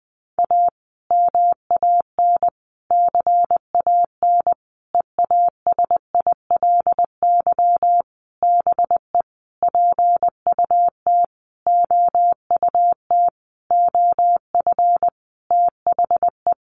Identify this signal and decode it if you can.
Morse code, heard as a keyed tone, about 20 wpm, 700 Hz. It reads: A MAN CAN EASILY BE PUT OUT OF THE
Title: Sentences from Top 900 Words 20wpm